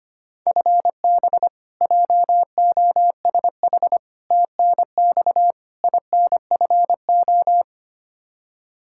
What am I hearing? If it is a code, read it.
Morse code, heard as a keyed tone, about 25 wpm, 700 Hz. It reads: FB JOSH TNX INFO
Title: Practice QSO - Exchange 3 - More Info - 25wpm